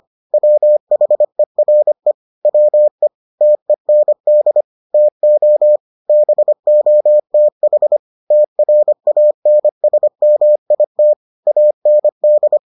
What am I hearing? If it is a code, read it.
Morse code, heard as a keyed tone, about 25 wpm, 600 Hz. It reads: WHERE WE TEND TO BOTH TRANSMIT AND